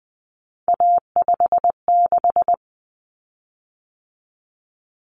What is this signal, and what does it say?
Morse code, heard as a keyed tone, about 20 wpm, 700 Hz. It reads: A56
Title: Common Cut Numbers - 3-Digit 20wpm